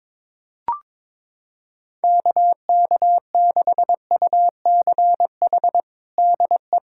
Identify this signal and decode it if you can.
Morse code, heard as a keyed tone, about 22 wpm, 700 Hz. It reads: KK6UCH DE
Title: USA Callsign DE Callsign 22wpm